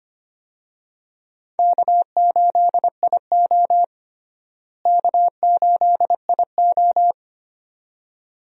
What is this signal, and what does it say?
Morse code, heard as a keyed tone, about 25 wpm, 700 Hz. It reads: K8IO K8IO
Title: USA Call Signs - 4 Characters - Repeated 2x - 25wpm